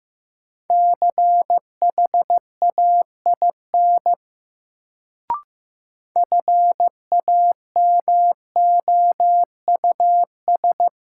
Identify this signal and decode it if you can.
Morse code, heard as a keyed tone, about 15 wpm, 700 Hz. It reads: CHAIN FAMOUS
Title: Sentences from Top 1500 Words - New Word Drill - 15wpm